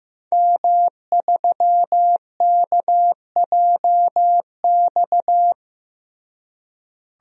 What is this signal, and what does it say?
Morse code, heard as a keyed tone, about 15 wpm, 700 Hz. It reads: M3KJX
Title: DXCC-100 Call Signs - 5 Characters - Repeated 3x - 15wpm